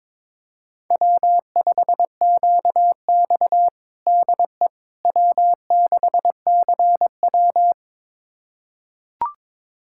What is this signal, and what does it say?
Morse code, heard as a keyed tone, about 22 wpm, 700 Hz. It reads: W5QX DE W6CW
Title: USA Callsign DE Callsign - 4 Characters - 22wpm